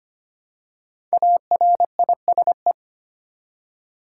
Morse code, heard as a keyed tone, about 25 wpm, 700 Hz. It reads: ARISE
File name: 5 Letter Words - First 3 Letters Given - 25wpm